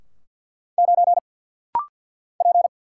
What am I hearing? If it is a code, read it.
Morse code, heard as a keyed tone, about 49 wpm, 700 Hz. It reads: 9 P
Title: Single Letter-Number - Rapid-Fire 50wpm